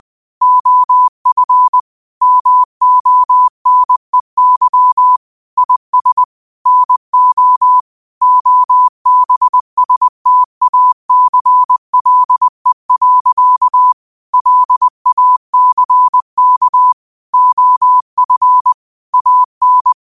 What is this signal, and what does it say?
Morse code, heard as a keyed tone, about 20 wpm, 1000 Hz. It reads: OF MONEY IS NO OBSTACLE. LACK OF AN
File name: qotdfiles / 20wpm